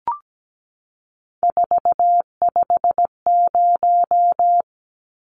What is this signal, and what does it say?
Morse code, heard as a keyed tone, about 17 wpm, 700 Hz. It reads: 450
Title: Numbers - 3-Digit 17wpm